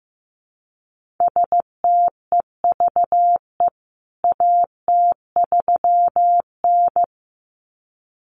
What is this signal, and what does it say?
Morse code, heard as a keyed tone, about 15 wpm, 700 Hz. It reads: STEVE AT3N
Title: CWT Exchange 15wpm